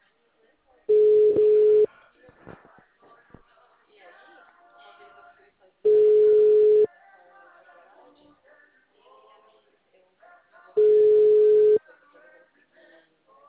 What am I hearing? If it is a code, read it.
Ringback tone (Europe, 425 Hz), with speech.
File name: out- -2011- - - .137